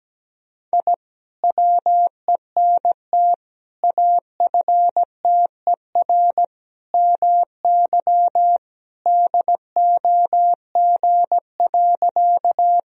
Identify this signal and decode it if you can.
Morse code, heard as a keyed tone, about 17 wpm, 700 Hz. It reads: I WENT AFTER MY DOG.
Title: Sentences from Top 1000 Words - Review Part 2 - 17wpm